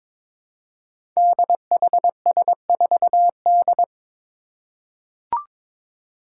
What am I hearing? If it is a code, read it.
Morse code, heard as a keyed tone, about 22 wpm, 700 Hz. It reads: DHS4D
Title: Five Challenging Letter-Number Combinations 22wpm